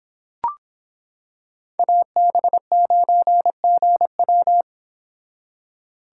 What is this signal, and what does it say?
Morse code, heard as a keyed tone, about 26 wpm, 700 Hz. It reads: AB9GW